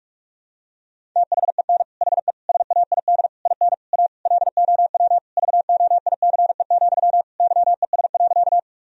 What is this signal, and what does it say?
Morse code, heard as a keyed tone, about 46 wpm, 700 Hz. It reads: THEN HE SAID IN A LOW VOICE, YES.
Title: Sentences from Top 1000 Words - Review Part 1 - 45wpm